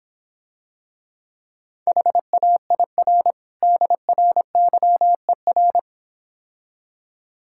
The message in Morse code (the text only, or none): HAIR DRYER